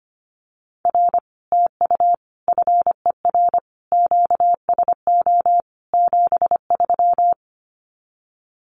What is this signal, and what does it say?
Morse code, heard as a keyed tone, about 25 wpm, 700 Hz. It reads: R TU FER QSO 73
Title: Common QSO Phrases Encore 25wpm